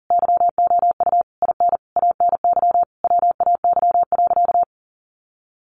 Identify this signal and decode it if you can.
Morse code, heard as a keyed tone, about 40 wpm, 700 Hz. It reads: YOU IN ANY WAY.
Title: Sentences from Top 1100 Words 40wpm